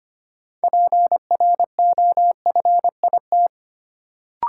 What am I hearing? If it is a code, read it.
Morse code, heard as a keyed tone, about 25 wpm, 700 Hz. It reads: PROFIT
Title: Sentences from Top 800 Words - New Word Drill - 25wpm